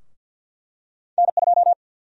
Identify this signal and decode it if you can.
Morse code, heard as a keyed tone, about 51 wpm, 700 Hz. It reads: N2
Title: USA Call Sign Prefixes - Repeated 2x - 50wpm